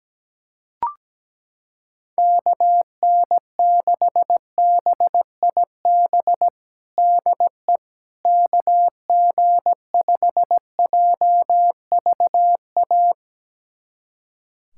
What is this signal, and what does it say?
Morse code, heard as a keyed tone, about 17 wpm, 700 Hz. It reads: KN6BIB DE KG5JVA
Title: USA Callsign DE Callsign - 6 Characters - 17wpm